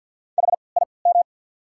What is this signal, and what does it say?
Morse code, heard as a keyed tone, about 48 wpm, spaced as for 28 wpm, 690 Hz. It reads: HIM
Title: 3x Character Spacing - Top 300 Words - 50wpm